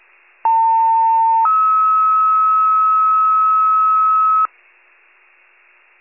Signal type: two-tone page

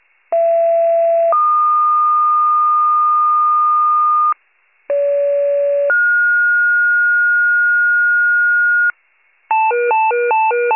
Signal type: two-tone page

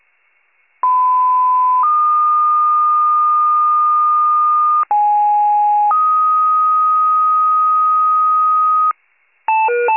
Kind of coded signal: two-tone page

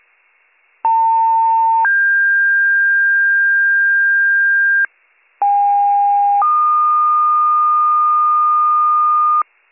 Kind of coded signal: two-tone page

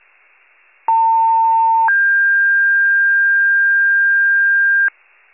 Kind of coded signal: two-tone page

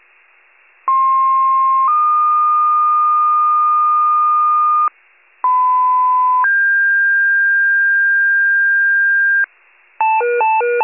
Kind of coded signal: two-tone page

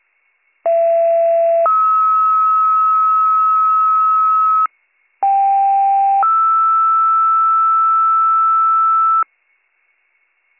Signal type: two-tone page

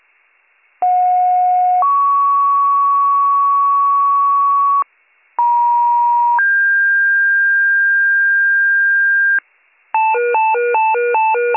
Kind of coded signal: two-tone page

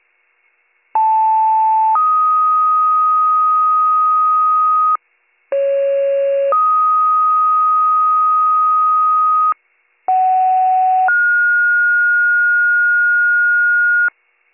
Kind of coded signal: two-tone page

